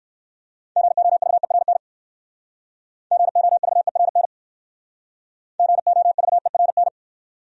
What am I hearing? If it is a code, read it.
Morse code, heard as a keyed tone, about 57 wpm, 690 Hz. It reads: GOVERN GOVERN GOVERN